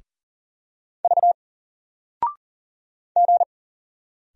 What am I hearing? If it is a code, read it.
Morse code, heard as a keyed tone, about 40 wpm, 700 Hz. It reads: V G
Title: Single Letter-Number Alt 40wpm